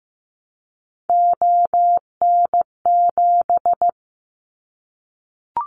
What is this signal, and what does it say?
Morse code, heard as a keyed tone, about 15 wpm, 700 Hz. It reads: ON7
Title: Callsign Prefixes for Top 100 DXCC - 15wpm